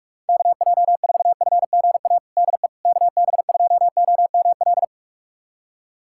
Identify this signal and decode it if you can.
Morse code, heard as a keyed tone, about 46 wpm, 700 Hz. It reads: KJ4FGA DE KB2OML
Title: USA Callsign DE Callsign - 6 Characters - 45wpm